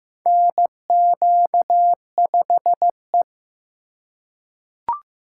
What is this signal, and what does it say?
Morse code, heard as a keyed tone, about 15 wpm, 700 Hz. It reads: NQ5E
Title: USA Call Signs - 4 Characters - 6x Character Spacing - 15wpm